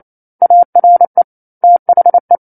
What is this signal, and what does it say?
Morse code, heard as a keyed tone, about 28 wpm, 700 Hz. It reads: ARE THE
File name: 09-Chapter-09 28 WPM